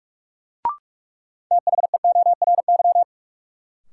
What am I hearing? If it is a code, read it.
Morse code, heard as a keyed tone, about 46 wpm, 700 Hz. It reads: THEORY
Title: Sentences from Top 600 Words - New Word Drill - 45wpm